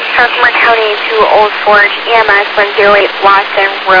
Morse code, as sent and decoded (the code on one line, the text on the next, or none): none